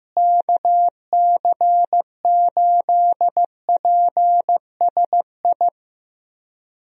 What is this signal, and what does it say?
Morse code, heard as a keyed tone, about 15 wpm, 700 Hz. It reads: KC8PSI
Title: USA Call Signs - 6 Characters - Repeated 3x - 15wpm